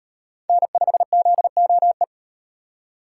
Morse code, heard as a keyed tone, about 38 wpm, 700 Hz. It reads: N5ZOE